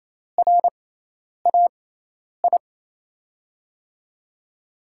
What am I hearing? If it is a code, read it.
Morse code, heard as a keyed tone, about 28 wpm, spaced as for 10 wpm, 700 Hz. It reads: RAI
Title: USA Call Signs - 6x Character Spacing - 28wpm